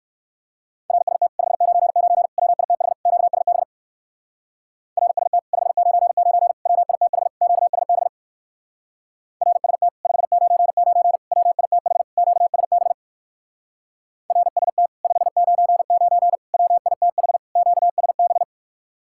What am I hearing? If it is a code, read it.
Morse code, heard as a keyed tone, about 56 wpm, 700 Hz. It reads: RST 599 WITH QSB RST 599 WITH QSB RST 599 WITH QSB RST 599 WITH QSB